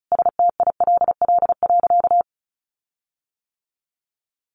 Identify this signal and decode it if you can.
Morse code, heard as a keyed tone, about 34 wpm, 700 Hz. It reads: STILL.
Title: Sentences from Top 1500 Words 35wpm